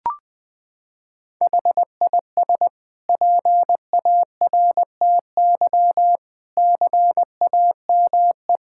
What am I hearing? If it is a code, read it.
Morse code, heard as a keyed tone, about 20 wpm, 700 Hz. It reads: HIS PARTY CAME
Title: Sentences from Top 1600 Words 20wpm